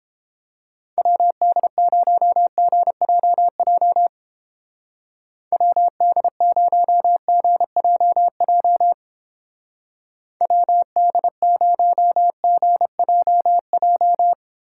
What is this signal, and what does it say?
Morse code, heard as a keyed tone, about 31 wpm, 700 Hz. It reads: WD0GJJ WD0GJJ WD0GJJ